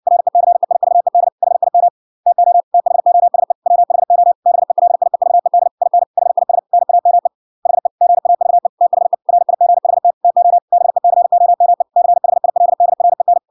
Morse code, heard as a keyed tone, about 56 wpm, 700 Hz. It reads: RECEIVED HIM TO THOSE WHO BELIEVED IN HIS NAME HE GAVE THE RIGHT TO BECOME CHILDREN